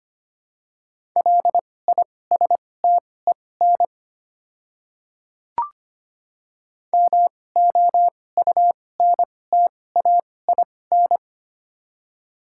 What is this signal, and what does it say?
Morse code, heard as a keyed tone, about 25 wpm, spaced as for 18 wpm, 700 Hz. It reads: LISTEN MOUNTAIN
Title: 2x Character Spacing - Top 500 Words - 25wpm